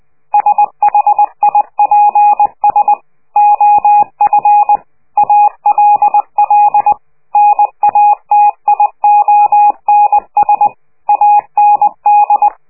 Morse code, heard as a keyed tone, about 20 wpm, 830 Hz. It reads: SHIPS OF ALL NATIONS AND
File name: worldcoastal / scheveningenradio / morsepch